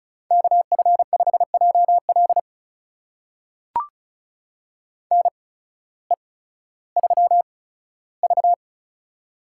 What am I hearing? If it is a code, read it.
Morse code, heard as a keyed tone, about 36 wpm, spaced as for 16 wpm, 700 Hz. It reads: KF5JL NE3V
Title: USA Call Signs - 8x Character Spacing - 35wpm